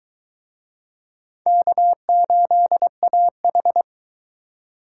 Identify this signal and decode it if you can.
Morse code, heard as a keyed tone, about 22 wpm, 700 Hz. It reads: K8AH